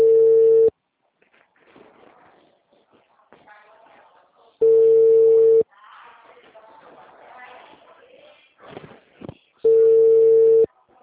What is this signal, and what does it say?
Ringback tone (China, 450 Hz).